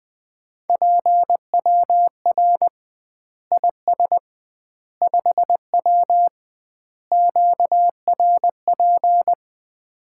Morse code, heard as a keyed tone, about 20 wpm, 700 Hz. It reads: PWR IS 5W QRP
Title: Common QSO Phrases - 2x Word-Spacing - 20wpm